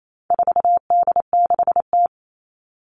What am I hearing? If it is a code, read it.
Morse code, heard as a keyed tone, about 28 wpm, 700 Hz. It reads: 4D6T
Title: All Cut Numbers - 4-Digit 28wpm